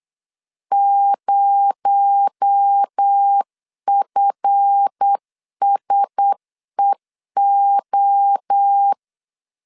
Morse code, heard as a keyed tone, about 8 wpm, 800 Hz. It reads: 0FSEO
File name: LEZIONI / 7 GF9Z / 7 GF 8wpm 3108